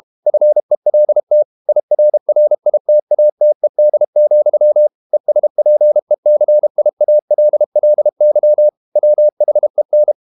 Morse code, heard as a keyed tone, about 32 wpm, 600 Hz. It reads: FELT IRRITATED, ESPECIALLY WHEN
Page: Never 31wpm | CW med Gnister